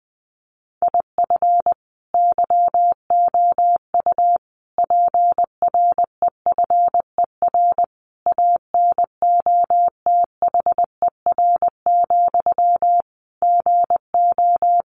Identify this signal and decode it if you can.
Morse code, heard as a keyed tone, about 20 wpm, 700 Hz. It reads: IF YOU PREFER ANOTHER, GO
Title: Sentences from Top 1100 Words 20wpm